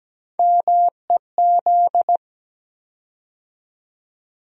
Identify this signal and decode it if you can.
Morse code, heard as a keyed tone, about 17 wpm, 700 Hz. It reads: MEZ
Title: USA Call Sign Postfixes - 3 Letters 17wpm